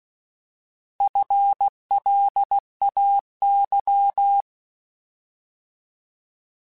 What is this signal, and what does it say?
Morse code, heard as a keyed tone, about 16 wpm, 800 Hz. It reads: FLAY
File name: Alphabetical / Alphabetical mp3s / Fs @15